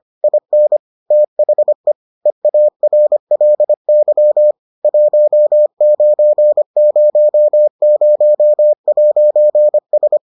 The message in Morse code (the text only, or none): IN THE EARLY 1900'S